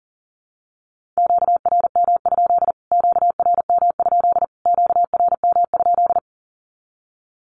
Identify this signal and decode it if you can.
Morse code, heard as a keyed tone, about 40 wpm, 700 Hz. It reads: QRM? QRM? QRM?
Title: Q-Codes Repeated 3x 40wpm